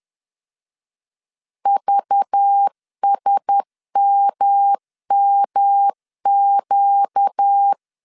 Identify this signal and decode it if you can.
Morse code, heard as a keyed tone, about 10 wpm, 800 Hz. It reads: VSMMQ
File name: CW / DET / 1 6 1500C S B 10wpm 4137